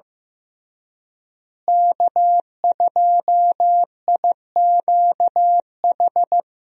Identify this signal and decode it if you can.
Morse code, heard as a keyed tone, about 16 wpm, 700 Hz. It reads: K2IQH
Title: USA Call Signs - 5 Characters - Repeated 3x - 15wpm